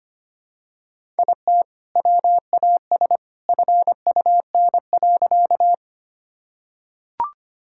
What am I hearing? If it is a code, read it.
Morse code, heard as a keyed tone, about 25 wpm, 700 Hz. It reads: IT WAS FUN.
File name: practice / 25wpm / Sentences with 2-Word Phrases - 25wpm